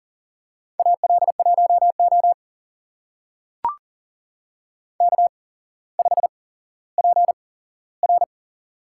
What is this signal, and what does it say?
Morse code, heard as a keyed tone, about 38 wpm, spaced as for 17 wpm, 700 Hz. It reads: AL1O K5PR
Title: USA Call Signs - 4 Characters - 8x Character Spacing - 40wpm